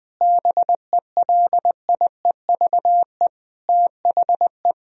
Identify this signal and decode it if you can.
Morse code, heard as a keyed tone, about 20 wpm, 700 Hz. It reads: BELIEVE THE
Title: Sentences from Top 1000 Words - Review Part 1 - 20wpm